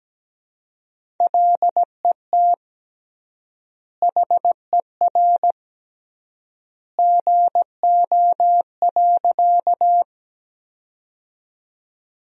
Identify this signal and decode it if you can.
Morse code, heard as a keyed tone, about 17 wpm, 700 Hz. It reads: LET HER GO.
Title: 3x Word-Spacing - Sentences from Top 300 Words - 17wpm